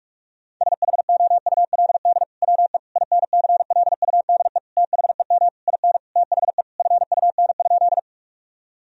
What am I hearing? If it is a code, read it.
Morse code, heard as a keyed tone, about 41 wpm, 700 Hz. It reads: SHOULD WE INCLUDE THEM IN THE FUN?
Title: Sentences from Top 1000 Words - Review Part 3 - 45wpm